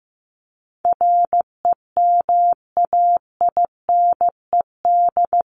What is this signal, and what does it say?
Morse code, heard as a keyed tone, about 15 wpm, 700 Hz. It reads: REMAINED
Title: ED Suffix - Top 100 Words with Base Words 15wpm